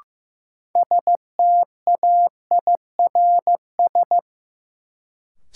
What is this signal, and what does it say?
Morse code, heard as a keyed tone, about 15 wpm, 700 Hz. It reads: STAIRS
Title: Sets of 5 Words Encore - New Word Drill - 15wpm